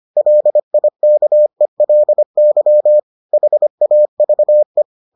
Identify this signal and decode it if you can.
Morse code, heard as a keyed tone, about 25 wpm, 600 Hz. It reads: LIKELY HAVE